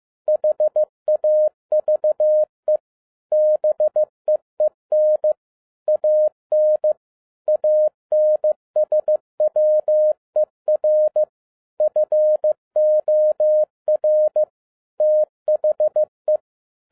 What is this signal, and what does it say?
Morse code, heard as a keyed tone, about 15 wpm, 610 Hz. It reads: HAVE BEEN AN ANSWER FOR THE